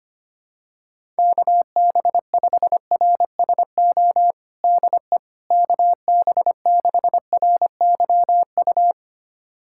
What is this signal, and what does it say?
Morse code, heard as a keyed tone, about 25 wpm, 700 Hz. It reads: KB5RSO DE KB6RYU
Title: USA Callsign DE Callsign - 6 Characters - 25wpm